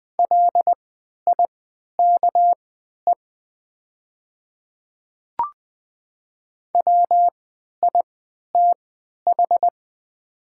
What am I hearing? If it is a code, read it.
Morse code, heard as a keyed tone, about 20 wpm, spaced as for 11 wpm, 700 Hz. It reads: LIKE WITH
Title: 3x Character Spacing - Top 200 Words - 20wpm